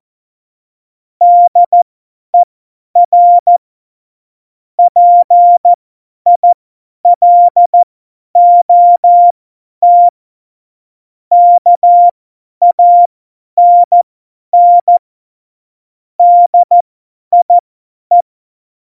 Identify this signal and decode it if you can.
Morse code, heard as a keyed tone, about 14 wpm, spaced as for 10 wpm, 700 Hz. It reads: DER PILOT KANN DIE
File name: Morsekurs / Witze / Witz 05. Januar 2021, 50 F70